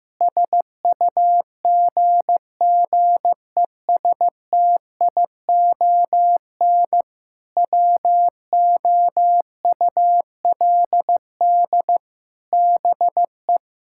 Morse code, heard as a keyed tone, about 16 wpm, 700 Hz. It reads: SUGGESTION WOULD BE
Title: Sentences from Top 1600 Words 15wpm